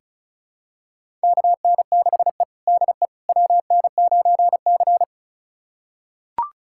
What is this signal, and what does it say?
Morse code, heard as a keyed tone, about 36 wpm, 700 Hz. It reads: KN6E DE WN9C
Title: USA Callsign DE Callsign - 4 Characters - 35wpm